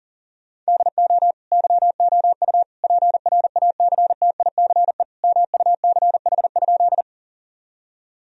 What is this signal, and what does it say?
Morse code, heard as a keyed tone, about 40 wpm, 700 Hz. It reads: DO YOU PRACTICE MUCH?
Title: Sentences from Top 1000 Words - Review Part 4 - 40wpm